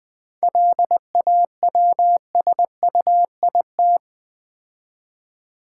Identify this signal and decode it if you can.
Morse code, heard as a keyed tone, about 20 wpm, 690 Hz. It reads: LAWSUIT
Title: 7 Letter Words - First 3 Letters Given - 20wpm